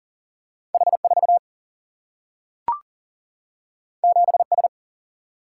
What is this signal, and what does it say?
Morse code, heard as a keyed tone, about 40 wpm, 700 Hz. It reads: H4 7S
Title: Single-Letter Single-Number 40wpm